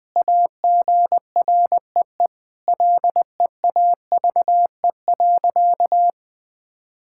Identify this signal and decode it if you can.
Morse code, heard as a keyed tone, about 20 wpm, 700 Hz. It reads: AGREE LEAVE.
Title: Sentences with 4-Word Phrases - 20wpm